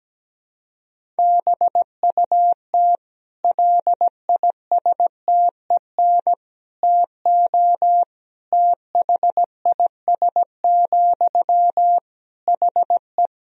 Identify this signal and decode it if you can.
Morse code, heard as a keyed tone, about 17 wpm, 700 Hz. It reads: BUT LISTEN TO THIS, HE
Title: Sentences from Top 400 Words and Abbreviations 17wpm